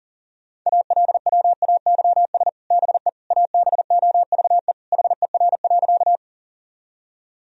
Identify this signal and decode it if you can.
Morse code, heard as a keyed tone, about 40 wpm, 700 Hz. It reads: ALWAYS BE ABOVE HER.
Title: Sentences from Top 1000 Words - Review Part 2 - 40wpm